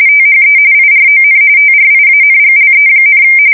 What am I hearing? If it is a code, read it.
RTTY, about 45.45 baud. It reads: THE IRON TRIANGLE,
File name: fsk170 / USS-New-Jersey